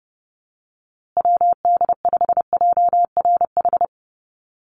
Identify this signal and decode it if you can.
Morse code, heard as a keyed tone, about 28 wpm, 700 Hz. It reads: WD5JRH
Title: USA Call Signs Repeated 2x 30wpm